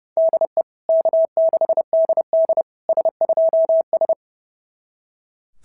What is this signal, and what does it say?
Morse code, heard as a keyed tone, about 28 wpm, 660 Hz. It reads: DE K6DD S2S
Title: SOTA QSO Practice 30wpm